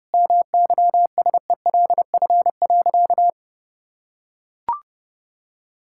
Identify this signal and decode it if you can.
Morse code, heard as a keyed tone, about 30 wpm, 700 Hz. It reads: MYSELF.
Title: Sentences from Top 1600 Words 30wpm